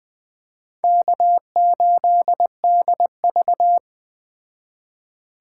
Morse code, heard as a keyed tone, about 20 wpm, 700 Hz. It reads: K8DV
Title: Speed Racing - USA 1x2 Call Signs 17wpm